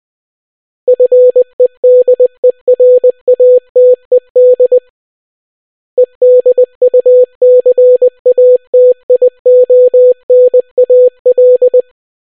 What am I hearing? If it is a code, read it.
Morse code, heard as a keyed tone, about 20 wpm, 500 Hz. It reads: FEDERATED EDUCATIONAL